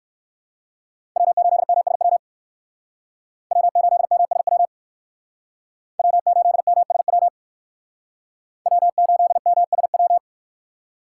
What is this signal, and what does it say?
Morse code, heard as a keyed tone, about 58 wpm, 700 Hz. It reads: W8MSW W8MSW W8MSW W8MSW